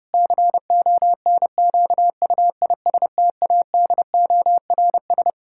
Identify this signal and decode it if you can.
Morse code, heard as a keyed tone, about 32 wpm, 700 Hz. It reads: CONQUISTADORS
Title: Pangram Challenge v2 - 30wpm